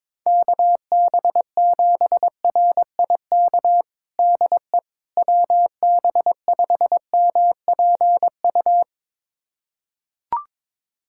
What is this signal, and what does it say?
Morse code, heard as a keyed tone, about 22 wpm, 700 Hz. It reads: KB7RIK DE WB5MPU
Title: USA Callsign DE Callsign - 6 Characters - 22wpm